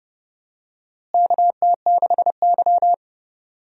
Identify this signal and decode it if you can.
Morse code, heard as a keyed tone, about 32 wpm, 700 Hz. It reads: KT6Y
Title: USA Call Signs - 4 Characters - Repeated 3x - 30wpm